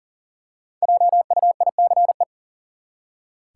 Morse code, heard as a keyed tone, about 40 wpm, 700 Hz. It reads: JUICE
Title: Five Letter Words - 40wpm